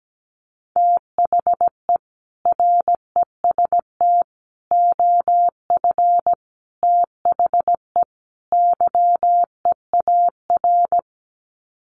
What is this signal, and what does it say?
Morse code, heard as a keyed tone, about 17 wpm, 700 Hz. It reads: THE REST OF THE YEAR
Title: Sets of 5 Words Encore 17wpm